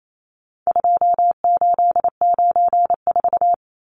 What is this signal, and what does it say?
Morse code, heard as a keyed tone, about 28 wpm, 700 Hz. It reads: 2894
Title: Numbers - 4-Digit 28wpm